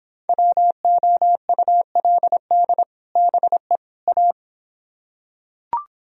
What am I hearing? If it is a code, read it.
Morse code, heard as a keyed tone, about 26 wpm, 700 Hz. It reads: WOULD BE A